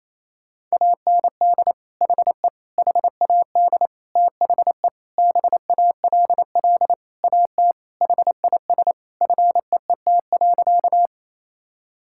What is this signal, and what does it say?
Morse code, heard as a keyed tone, about 27 wpm, 700 Hz. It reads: AND HE HAD THE BALL AT HIS FEET.
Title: Sentences from Top 1000 Words - Review Part 6 - 28wpm